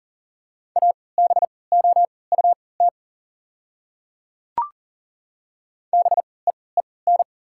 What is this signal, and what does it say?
Morse code, heard as a keyed tone, about 40 wpm, spaced as for 23 wpm, 690 Hz. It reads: ABOUT BEEN
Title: 3x Character Spacing - Top 100 Words - 40wpm